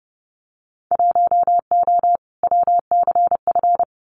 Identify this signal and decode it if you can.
Morse code, heard as a keyed tone, about 30 wpm, 700 Hz. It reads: 1O WCF
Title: Winter Field Day Exchanges 30wpm